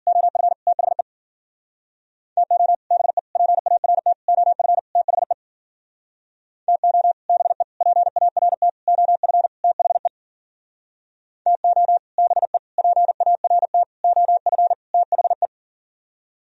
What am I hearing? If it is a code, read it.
Morse code, heard as a keyed tone, about 48 wpm, 700 Hz. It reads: OF THE TO BE PART OF THE TO BE PART OF THE TO BE PART OF THE
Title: Speed-Racing - Sets of 5 Words 40wpm